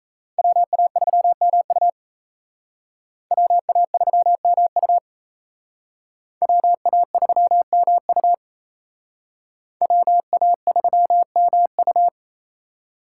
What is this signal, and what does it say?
Morse code, heard as a keyed tone, about 35 wpm, 700 Hz. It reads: WA3MU WA3MU WA3MU WA3MU
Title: Speed Racing - USA 2x2 Call Signs 28wpm